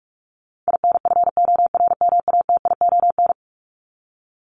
Morse code, heard as a keyed tone, about 44 wpm, 700 Hz. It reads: INFORMATION
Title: Top 501-750 Words 45wpm